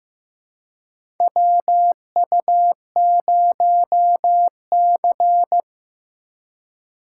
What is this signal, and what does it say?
Morse code, heard as a keyed tone, about 16 wpm, 700 Hz. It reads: WU0C
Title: USA Call Signs - 3x Character Spacing - 15wpm